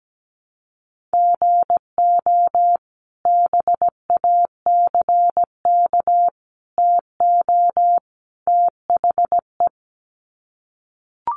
Morse code, heard as a keyed tone, about 17 wpm, 700 Hz. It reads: GO BACK TO THE
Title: Sets of 4 Words 17wpm